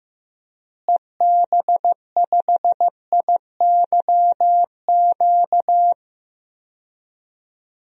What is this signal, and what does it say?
Morse code, heard as a keyed tone, about 15 wpm, 700 Hz. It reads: EB5IYQ
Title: DXCC-100 Call Signs - 6 Characters - Repeated 2x - 15wpm